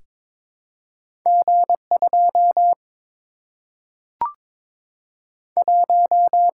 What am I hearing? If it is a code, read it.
Morse code, heard as a keyed tone, about 22 wpm, 700 Hz. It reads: G 2 1
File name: practice / 22wpm / Two Letter-Number Combinations - 22wpm